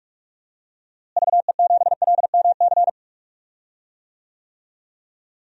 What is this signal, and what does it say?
Morse code, heard as a keyed tone, about 46 wpm, 700 Hz. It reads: VE7LMC
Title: DXCC-100 Call Signs - 6 Characters - Repeated 3x - 45wpm